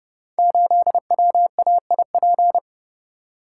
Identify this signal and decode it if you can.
Morse code, heard as a keyed tone, about 30 wpm, 700 Hz. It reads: 8WAIP
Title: Code Groups - 5 Letter:Numbers - 30wpm